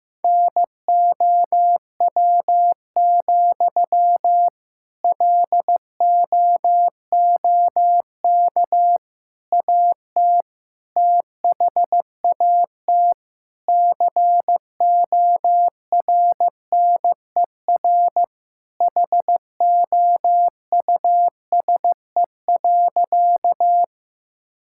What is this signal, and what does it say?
Morse code, heard as a keyed tone, about 15 wpm, 700 Hz. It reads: NOW, LOOK AT THAT CORNER HOUSE.
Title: Sentences from Top 1600 Words 15wpm